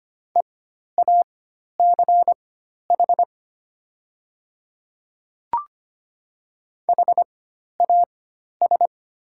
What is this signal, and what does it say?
Morse code, heard as a keyed tone, about 25 wpm, spaced as for 12 wpm, 700 Hz. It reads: EACH HAS
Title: 4x Character Spacing - Top 100 Words - 25wpm